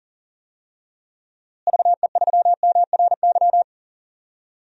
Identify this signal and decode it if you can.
Morse code, heard as a keyed tone, about 40 wpm, 700 Hz. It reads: VE3MRY
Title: DXCC-100 Call Signs - 6 Characters - Repeated 3x - 40wpm